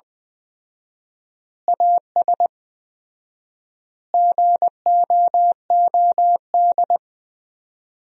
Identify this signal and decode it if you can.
Morse code, heard as a keyed tone, about 21 wpm, 700 Hz. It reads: AS GOOD
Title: 4x Word-Spacing - Sentences from Top 200 Words - 20wpm